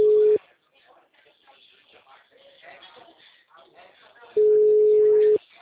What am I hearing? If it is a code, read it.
Ringback tone (Europe, 425 Hz), with speech.